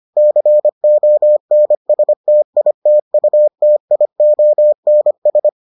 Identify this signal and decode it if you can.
Morse code, heard as a keyed tone, about 25 wpm, 600 Hz. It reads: CONSTITUTIONS